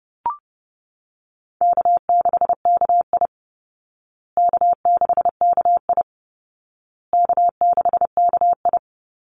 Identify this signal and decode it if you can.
Morse code, heard as a keyed tone, about 30 wpm, 700 Hz. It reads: K6KI K6KI K6KI
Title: Easy Call Signs Repeated 3x - 30wpm